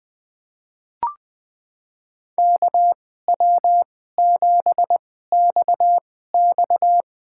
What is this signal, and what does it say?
Morse code, heard as a keyed tone, about 20 wpm, spaced as for 14 wpm, 700 Hz. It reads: KW7XX
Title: USA Call Signs - 5 Characters - 2x Character Spacing - 20wpm